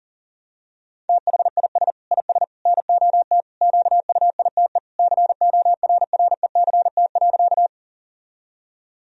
Morse code, heard as a keyed tone, about 40 wpm, 700 Hz. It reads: THIS IS NOT QUITE CORRECT.
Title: Sentences from Top 1000 Words - Review Part 2 - 40wpm